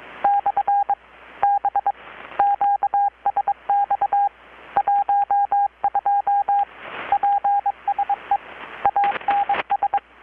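Morse code, heard as a keyed tone, about 22 wpm, 810 Hz, with radio noise. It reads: / B QSX 12 PSE ANS
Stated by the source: Morse from KPH's Night of Nights XX maritime radiotelegraphy event, received in Chiapas, Mexico about 3,800 km away on a Kenwood R-600 with a grounded 10 m long-wire antenna.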